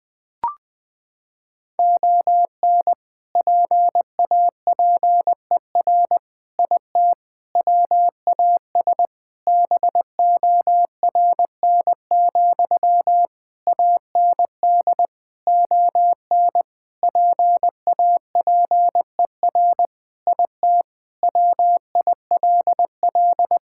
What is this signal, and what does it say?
Morse code, heard as a keyed tone, about 20 wpm, 700 Hz. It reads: ON PAPER IT WAS BORN, AND ON PAPER IT WILL
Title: Sentences from Top 1600 Words 20wpm